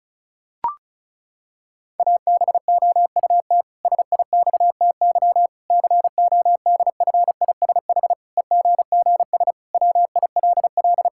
Morse code, heard as a keyed tone, about 34 wpm, 700 Hz. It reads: ABOUT SIXTY CODFISH EGGS WILL
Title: Pangram Challenge v2 - 35wpm